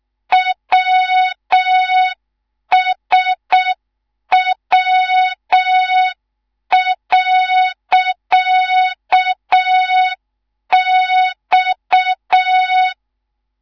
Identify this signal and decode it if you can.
Morse code, heard as a keyed tone, about 6 wpm, 3700 Hz. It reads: WSW.X
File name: Morze-HA2MN / hang / 19 GYAKORLAT